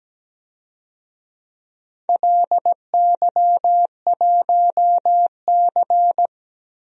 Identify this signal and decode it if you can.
Morse code, heard as a keyed tone, about 17 wpm, 700 Hz. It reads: LY1C
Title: DXCC-100 Call Signs - 4 Characters - Repeated 3x - 17wpm